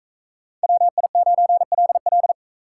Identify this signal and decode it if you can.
Morse code, heard as a keyed tone, about 40 wpm, 700 Hz. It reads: WI9LL